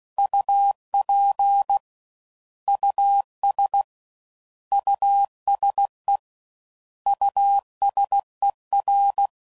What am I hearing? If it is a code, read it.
Morse code, heard as a keyed tone, about 16 wpm, 800 Hz. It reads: UP US USE USER
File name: Prefixes / By First Letter / Us